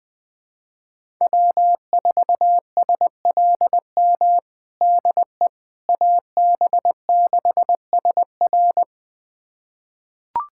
Morse code, heard as a keyed tone, about 21 wpm, 700 Hz. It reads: W4SLM DE AB6SR
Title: USA Callsign DE Callsign - 5 Characters - 20wpm